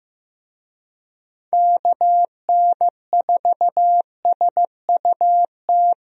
Morse code, heard as a keyed tone, about 15 wpm, 700 Hz. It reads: KN4SUT
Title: USA Call Signs - 6 Characters - Repeated 2x - 15wpm